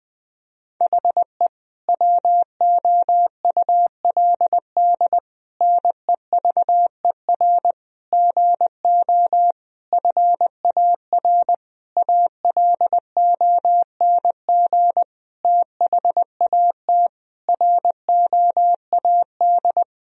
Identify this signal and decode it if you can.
Morse code, heard as a keyed tone, about 20 wpm, 700 Hz. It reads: HE WOULD NEVER GO FAR ALONG THAT ROAD
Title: Sentences from Top 1000 Words - Review Part 2 - 20wpm